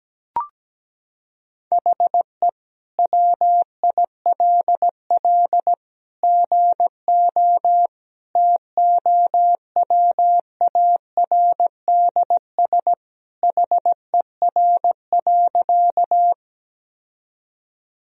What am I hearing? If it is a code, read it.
Morse code, heard as a keyed tone, about 17 wpm, 700 Hz. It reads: HE WILL GO TOWARDS HER.
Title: Sentences from Top 1000 Words - Review Part 2 - 17wpm